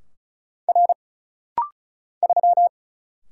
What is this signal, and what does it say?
Morse code, heard as a keyed tone, about 36 wpm, 700 Hz. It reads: R 3
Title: Single Letter-Number - Rapid-Fire 35wpm